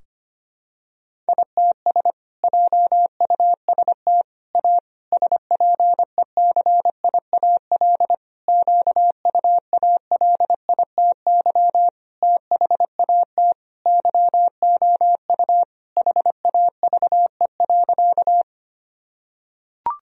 Morse code, heard as a keyed tone, about 25 wpm, 700 Hz. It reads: ITS JUST A SPECIAL QUALITY THAT YOU HAVE.
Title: Sentences from Top 1000 Words - Review Part 4 - 25wpm